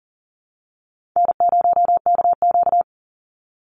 Morse code, heard as a keyed tone, about 40 wpm, 700 Hz. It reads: N0KQ
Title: Easy Call Signs - 40wpm